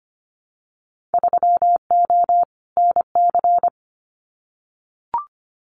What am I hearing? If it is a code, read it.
Morse code, heard as a keyed tone, about 25 wpm, 700 Hz. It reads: 3O NC
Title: Winter Field Day Exchanges 25wpm